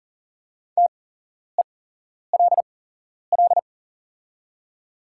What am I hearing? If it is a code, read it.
Morse code, heard as a keyed tone, about 40 wpm, spaced as for 11 wpm, 690 Hz. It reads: TELL
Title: 8x Character Spacing - Top 300 Words - 40wpm